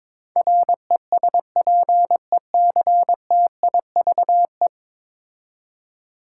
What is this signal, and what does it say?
Morse code, heard as a keyed tone, about 22 wpm, 700 Hz. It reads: RESPECTIVE
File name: 10 Letter Words - First 3 Letters Given - 22wpm